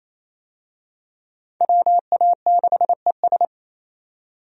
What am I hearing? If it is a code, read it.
Morse code, heard as a keyed tone, about 28 wpm, 700 Hz. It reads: WA6ES
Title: USA Call Signs - 5 Characters - 28wpm